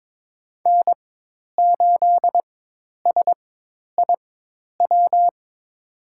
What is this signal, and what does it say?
Morse code, heard as a keyed tone, about 22 wpm, spaced as for 10 wpm, 700 Hz. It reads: N8SIW
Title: USA Call Signs - 5 Characters - 4x Character Spacing - 22wpm